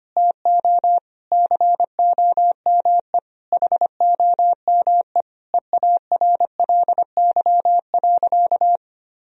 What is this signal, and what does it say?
Morse code, heard as a keyed tone, about 25 wpm, 700 Hz. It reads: TO COME HOME EARLY.
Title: Sentences from Top 1000 Words - Review Part 4 - 25wpm